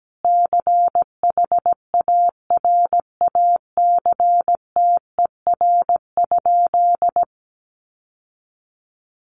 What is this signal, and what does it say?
Morse code, heard as a keyed tone, about 17 wpm, 700 Hz. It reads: CHARACTER?
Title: Sentences from Top 1200 Words 17wpm